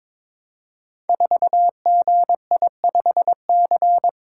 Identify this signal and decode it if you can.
Morse code, heard as a keyed tone, about 21 wpm, 700 Hz. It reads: 4GI5C
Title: Code Groups - 5 Letter:Numbers - 22wpm